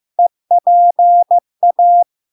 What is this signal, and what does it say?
Morse code, heard as a keyed tone, about 15 wpm, 700 Hz. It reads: EPA
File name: LARK-FD-2022-Log-in-CW-15wpm-700-Hz